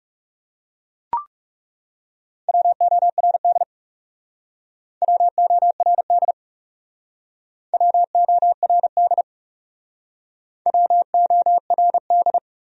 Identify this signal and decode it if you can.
Morse code, heard as a keyed tone, about 38 wpm, 700 Hz. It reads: WORD WORD WORD WORD